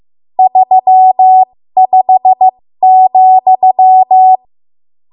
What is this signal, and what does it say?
Morse code, heard as a keyed tone, about 15 wpm, spaced as for 13 wpm, 750 Hz. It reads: 35,
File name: Morse / 13WPM